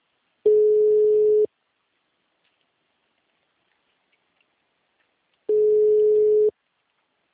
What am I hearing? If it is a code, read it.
Ringback tone (Europe, 425 Hz).